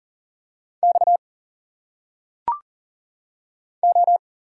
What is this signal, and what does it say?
Morse code, heard as a keyed tone, about 40 wpm, 700 Hz. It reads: X O
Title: Single Letters - Phonetic Pronunciation 40wpm